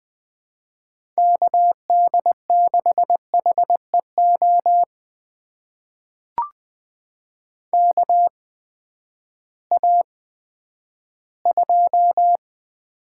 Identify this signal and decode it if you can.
Morse code, heard as a keyed tone, about 20 wpm, spaced as for 11 wpm, 700 Hz. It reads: KD6HEO KA2
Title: USA Call Signs - 6 Characters - 8x Character Spacing - 20wpm